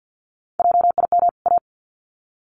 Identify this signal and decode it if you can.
Morse code, heard as a keyed tone, about 48 wpm, 700 Hz. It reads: JIM A
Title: CWT Exchange 50wpm